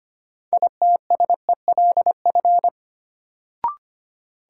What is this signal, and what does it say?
Morse code, heard as a keyed tone, about 25 wpm, 700 Hz. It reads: ITSELF
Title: Sentences from Top 800 Words - New Word Drill - 25wpm